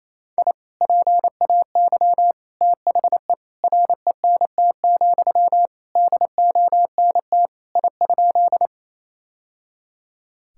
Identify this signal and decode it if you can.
Morse code, heard as a keyed tone, about 28 wpm, 700 Hz. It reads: I PAY THE RENT, DONT I?
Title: Sentences from Top 1000 Words - Review Part 5 - 28wpm